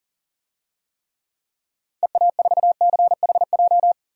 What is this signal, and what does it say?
Morse code, heard as a keyed tone, about 40 wpm, 700 Hz. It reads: EA4CHJ
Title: DXCC-100 Call Signs - 6 Characters - 40wpm